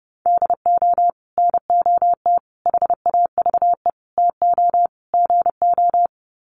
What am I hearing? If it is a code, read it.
Morse code, heard as a keyed tone, about 30 wpm, 700 Hz. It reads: DO NOT HAVE TO GO
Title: Sets of 5 Words 30wpm